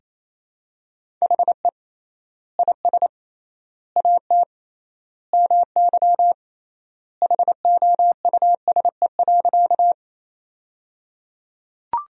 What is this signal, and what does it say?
Morse code, heard as a keyed tone, about 28 wpm, 700 Hz. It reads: HE IS AT MY HOUSE.
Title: 3x Word-Spacing - Sentences from Top 200 Words - 28wpm